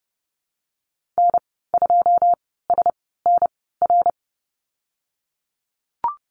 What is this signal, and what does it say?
Morse code, heard as a keyed tone, about 30 wpm, spaced as for 17 wpm, 700 Hz. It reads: N2SNR
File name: Lesson 19g - Letter w - Call Signs 30wpm